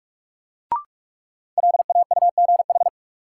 Morse code, heard as a keyed tone, about 47 wpm, 700 Hz. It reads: LAUGH
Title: Sentences from Top 500 Words - New Word Drill - 45wpm